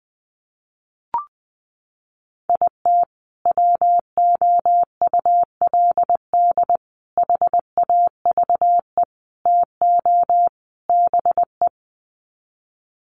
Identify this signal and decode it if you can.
Morse code, heard as a keyed tone, about 20 wpm, 700 Hz. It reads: IT WOULD HAVE TO BE
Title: Sets of 5 Words Encore 20wpm